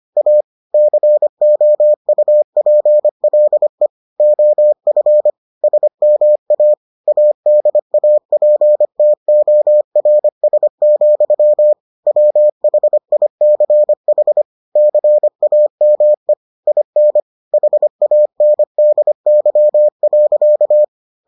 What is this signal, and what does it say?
Morse code, heard as a keyed tone, about 25 wpm, 600 Hz. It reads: A COUPLE OF SMA ADAPTORS, WHICH CAME IN HANDY.